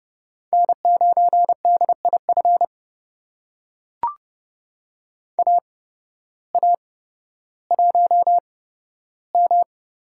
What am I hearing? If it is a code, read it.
Morse code, heard as a keyed tone, about 29 wpm, spaced as for 14 wpm, 700 Hz. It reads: N9DIF AA1M
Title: USA Call Signs - 5 Characters - 8x Character Spacing - 30wpm